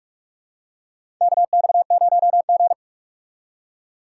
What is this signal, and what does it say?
Morse code, heard as a keyed tone, about 43 wpm, 700 Hz. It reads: KX0G
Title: USA Call Signs - 4 Characters - 2x Character Spacing - 45wpm